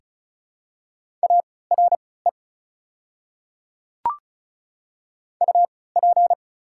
Morse code, heard as a keyed tone, about 36 wpm, spaced as for 20 wpm, 690 Hz. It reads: ARE UP
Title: 3x Character Spacing - Top 100 Words - 35wpm